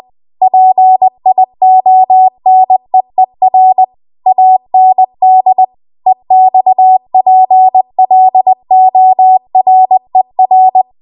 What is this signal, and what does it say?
Morse code, heard as a keyed tone, about 20 wpm, 750 Hz. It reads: PIONEER AND EXPLORER